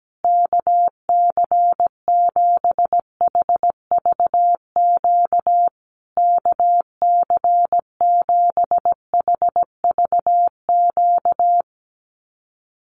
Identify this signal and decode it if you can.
Morse code, heard as a keyed tone, about 17 wpm, 700 Hz. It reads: KC7HVQ KC7HVQ
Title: USA Call Signs Repeated 3x 17wpm